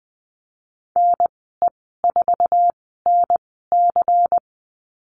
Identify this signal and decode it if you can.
Morse code, heard as a keyed tone, about 20 wpm, spaced as for 15 wpm, 700 Hz. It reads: NE4NC
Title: 147 Lesson 24g - Letter y - Call Signs 20wpm